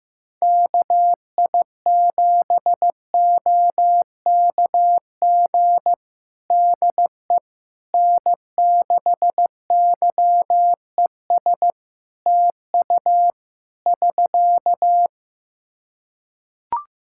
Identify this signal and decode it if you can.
Morse code, heard as a keyed tone, about 15 wpm, 700 Hz. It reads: KI7OKG DE N6YES TU <SK>
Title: Practice QSO - Exchange 4 - Ending - 15wpm